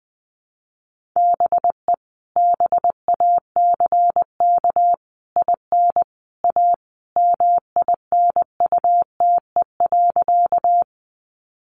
Morse code, heard as a keyed tone, about 20 wpm, 700 Hz. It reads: BE BACK IN A MINUTE.
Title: Sentences from Top 500 Words 20wpm